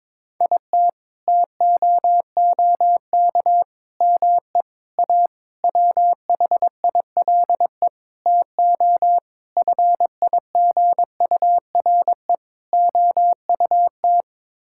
Morse code, heard as a keyed tone, about 22 wpm, 700 Hz. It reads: IT TOOK ME A WHILE TO FIGURE OUT
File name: practice / 22wpm / Sets of 8 Words - 22wpm